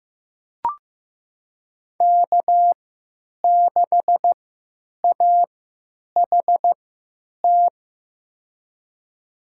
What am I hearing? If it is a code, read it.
Morse code, heard as a keyed tone, about 15 wpm, spaced as for 9 wpm, 700 Hz. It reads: K6AHT
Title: USA Call Signs - 3x Character Spacing - 15wpm